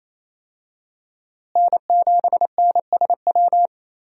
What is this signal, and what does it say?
Morse code, heard as a keyed tone, about 28 wpm, 700 Hz. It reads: N7NSW
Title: USA Call Signs - 5 Characters - Repeated 2x - 28wpm